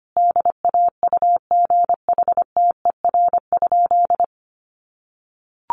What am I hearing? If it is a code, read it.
Morse code, heard as a keyed tone, about 25 wpm, 700 Hz. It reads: DAUGHTER?
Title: Sentences from Top 1500 Words 25wpm